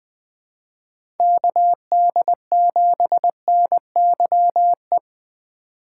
Morse code, heard as a keyed tone, about 20 wpm, 700 Hz. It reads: KD7NYE
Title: USA Call Signs - 6 Characters - 4x Character Spacing - 20wpm